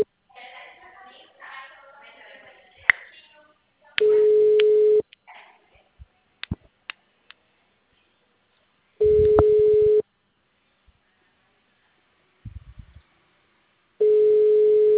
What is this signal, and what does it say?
Ringback tone (Europe, 425 Hz).